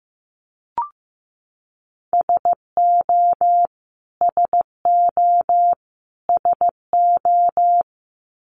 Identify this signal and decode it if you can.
Morse code, heard as a keyed tone, about 15 wpm, 700 Hz. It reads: SO SO SO
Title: Top 200 Words Repeated 15wpm